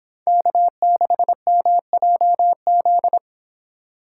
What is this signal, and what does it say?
Morse code, heard as a keyed tone, about 27 wpm, 700 Hz. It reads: K6MJZ
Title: Speed Racing - USA 1x3 Call Signs 22wpm